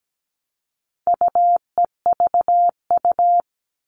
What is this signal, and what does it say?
Morse code, heard as a keyed tone, about 17 wpm, 700 Hz. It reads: UEVU
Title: All Cut Numbers - 4-Digit 17wpm